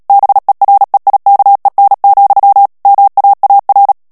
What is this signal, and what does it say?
Morse code, heard as a keyed tone, about 37 wpm, 800 Hz. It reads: BEREIKEN, MAAR
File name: morse mp3 / HOLLANDAIS texte clair 2 vitesse 35 mots minute